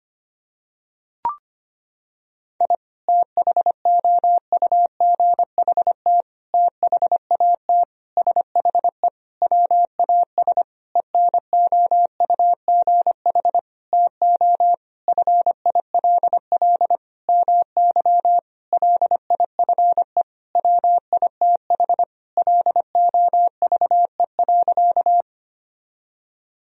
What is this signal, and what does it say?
Morse code, heard as a keyed tone, about 25 wpm, 700 Hz. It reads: I THOUGHT THAT SHE WAS ENOUGH TO FILL MY LIFE WITH LOVE.
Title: Sentences from Top 1000 Words - Review Part 3 - 25wpm